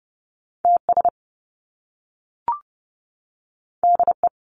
Morse code, heard as a keyed tone, about 30 wpm, 700 Hz. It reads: TS DE
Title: Top Two Letter Combinations in Words 30wpm